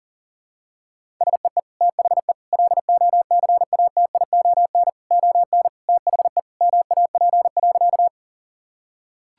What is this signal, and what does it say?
Morse code, heard as a keyed tone, about 40 wpm, 700 Hz. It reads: SEE THE LOCATION ON THE MAP.
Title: Sentences from Top 1000 Words - Review Part 6 - 40wpm